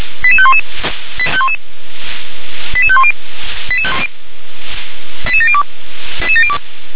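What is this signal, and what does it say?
Five-tone selective call (ZVEI), heard as quick runs of five tones, tones of about 70 ms. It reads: 88511 88510 88702
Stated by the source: ZVEI five-tone selective call received on an SDR.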